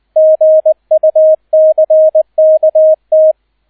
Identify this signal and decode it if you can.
Morse code, heard as a keyed tone, about 20 wpm, 630 Hz, with radio noise. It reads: GUCKT